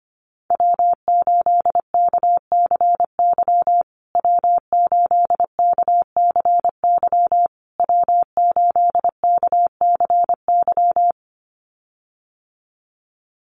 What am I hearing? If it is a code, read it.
Morse code, heard as a keyed tone, about 25 wpm, 700 Hz. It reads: W8KCY W8KCY W8KCY
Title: USA Call Signs Repeated 3x 25wpm